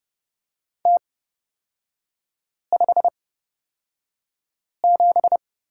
Morse code, heard as a keyed tone, about 30 wpm, spaced as for 5 wpm, 700 Hz. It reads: T57
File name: Lesson 37e - Letter x - Three Characters 30wpm